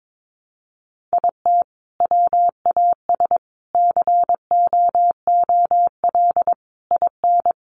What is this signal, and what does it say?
Morse code, heard as a keyed tone, about 22 wpm, 700 Hz. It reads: IT WAS COOL IN
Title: Sentences from Top 1200 Words 22wpm